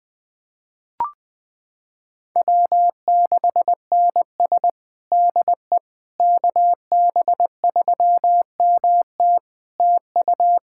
Morse code, heard as a keyed tone, about 21 wpm, 700 Hz. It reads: W6NS DE KB3MT TU
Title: Practice QSO - Exchange 4 - Ending - 20wpm